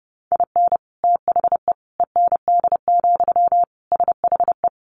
Morse code, heard as a keyed tone, about 30 wpm, 700 Hz. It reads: IN THE END, SHE
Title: Additional Easy Phrases 30wpm